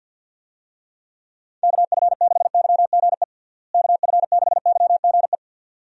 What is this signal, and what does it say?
Morse code, heard as a keyed tone, about 50 wpm, 700 Hz. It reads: KF6YGE KF6YGE
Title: USA Call Signs - 6 Characters - Repeated 2x - 50wpm